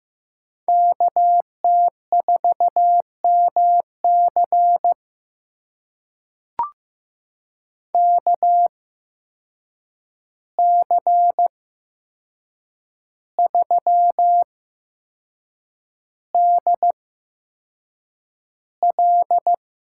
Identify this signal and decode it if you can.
Morse code, heard as a keyed tone, about 15 wpm, spaced as for 6 wpm, 700 Hz. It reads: KT4MC KC3DL
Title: USA Call Signs - 8x Character Spacing - 15wpm